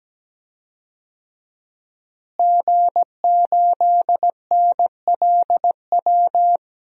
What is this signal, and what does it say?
Morse code, heard as a keyed tone, about 17 wpm, 700 Hz. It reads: G8NLW
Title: DXCC-100 Call Signs - 5 Characters - Repeated 2x - 17wpm